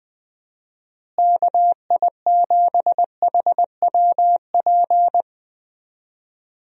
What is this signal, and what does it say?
Morse code, heard as a keyed tone, about 20 wpm, 700 Hz. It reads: KI7HWP
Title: USA Call Signs - 6 Characters - Repeated 3x - 20wpm